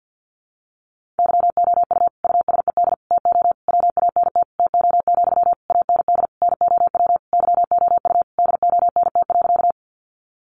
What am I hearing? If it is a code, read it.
Morse code, heard as a keyed tone, about 50 wpm, 700 Hz. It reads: YOU USED TO WANT TO, AND NOW YOU DONT.
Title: Sentences from Top 600 Words 50wpm